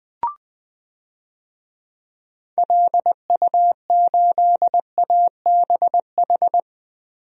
Morse code, heard as a keyed tone, about 20 wpm, 700 Hz. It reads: LU8ABH
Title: DXCC-100 Call Signs - 6 Characters - Repeated 2x - 20wpm